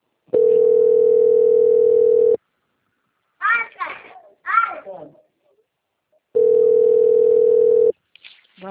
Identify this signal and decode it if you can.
Ringback tone (North America), 440 + 480 Hz, with speech.
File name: out- - - - - .21241